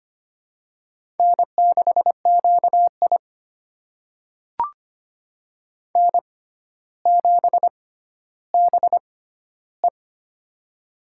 Morse code, heard as a keyed tone, about 25 wpm, spaced as for 13 wpm, 700 Hz. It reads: N6QI N7BE
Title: USA Call Signs - 6x Character Spacing - 25wpm